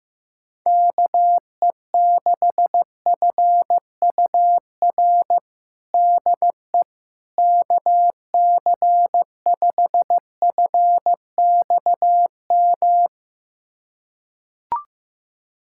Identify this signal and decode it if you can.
Morse code, heard as a keyed tone, about 15 wpm, 700 Hz. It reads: KE6FUR DE KC5FXM
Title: USA Callsign DE Callsign 15wpm